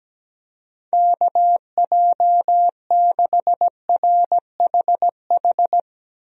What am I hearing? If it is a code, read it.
Morse code, heard as a keyed tone, about 17 wpm, 700 Hz. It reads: KJ6RHH
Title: USA Call Signs - 6 Characters - 17wpm